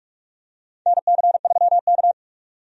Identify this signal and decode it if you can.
Morse code, heard as a keyed tone, about 42 wpm, 700 Hz. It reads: NC3K
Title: USA Call Signs - 4 Characters - 45wpm